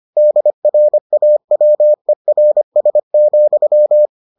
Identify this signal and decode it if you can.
Morse code, heard as a keyed tone, about 25 wpm, 600 Hz. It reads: DRAWERS,